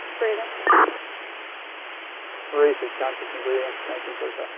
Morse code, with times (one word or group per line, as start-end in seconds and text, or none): none